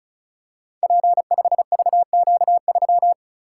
Morse code, heard as a keyed tone, about 36 wpm, 700 Hz. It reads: P5VQ3
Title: Code Groups - 5 Letter:Numbers - 35wpm